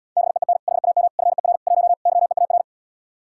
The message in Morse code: BEEN HAM FER 1 YEAR

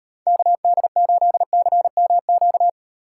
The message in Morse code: KD8CMQ

